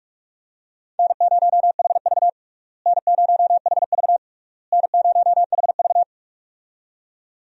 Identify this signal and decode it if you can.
Morse code, heard as a keyed tone, about 45 wpm, 700 Hz. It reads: N0HV N0HV N0HV
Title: USA Call Signs - 4 Characters - Repeated 3x - 45wpm